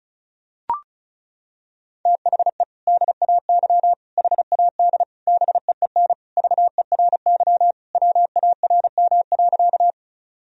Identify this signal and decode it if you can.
Morse code, heard as a keyed tone, about 36 wpm, 700 Hz. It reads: THE DAY HAD BEEN VERY WARM.
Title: Sentences from Top 1000 Words - Review Part 2 - 35wpm